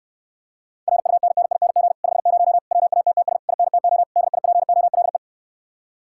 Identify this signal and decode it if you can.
Morse code, heard as a keyed tone, about 69 wpm, 700 Hz. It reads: RUNNING 50 WATTS INTO DIPOLE